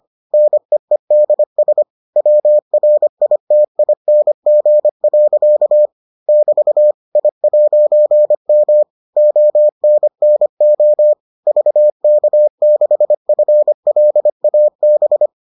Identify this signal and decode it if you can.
Morse code, heard as a keyed tone, about 25 wpm, 600 Hz. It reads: NEEDS WRITING. <BT> I'M ONNO VK6FLAB